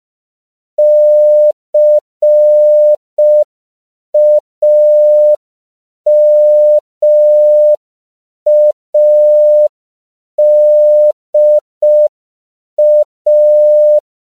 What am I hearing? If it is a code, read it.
Morse code, heard as a keyed tone, about 5 wpm, 600 Hz. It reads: CAMADA
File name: CW / 5PPM